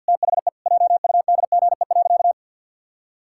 Morse code, heard as a keyed tone, about 52 wpm, 700 Hz. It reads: THE JUDGE.
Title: Pangram Challenge v2 - 50wpm